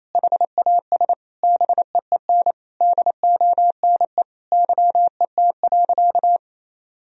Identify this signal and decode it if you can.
Morse code, heard as a keyed tone, about 28 wpm, 700 Hz. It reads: HAS BEEN DONE YET.
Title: Sentences from Top 1000 Words - Review Part 2 - 28wpm